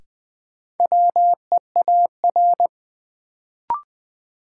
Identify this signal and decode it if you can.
Morse code, heard as a keyed tone, about 20 wpm, 700 Hz. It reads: WEAR
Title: Sentences from Top 1000 Words - Review Part 6 - New Word Drill - 20wpm